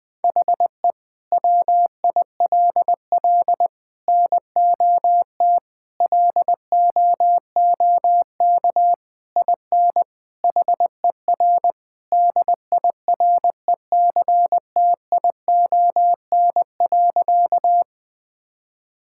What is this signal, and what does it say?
Morse code, heard as a keyed tone, about 21 wpm, 700 Hz. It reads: HE WILL NOT LOOK IN HER DIRECTION.
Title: Sentences from Top 1000 Words - Review Part 5 - 20wpm